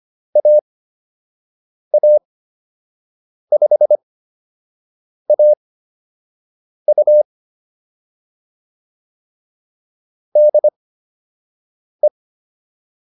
Morse code, heard as a keyed tone, about 25 wpm, spaced as for 6 wpm, 600 Hz. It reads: AA5AU DE